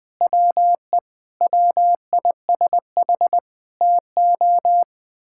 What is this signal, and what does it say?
Morse code, heard as a keyed tone, about 20 wpm, 700 Hz. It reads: WE WISH TO
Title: Sentences with 7-Word Phrases - 20wpm